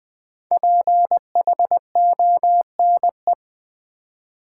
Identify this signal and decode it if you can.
Morse code, heard as a keyed tone, about 20 wpm, 700 Hz. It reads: PHONE
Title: Sentences from Top 1100 Words - New Word Drill v2 - 20wpm